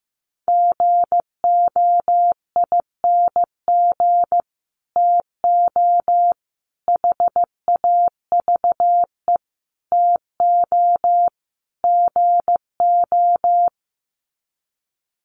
Morse code, heard as a keyed tone, about 15 wpm, 700 Hz. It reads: GOING TO HAVE TO GO
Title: Sets of 5 Words Encore 15wpm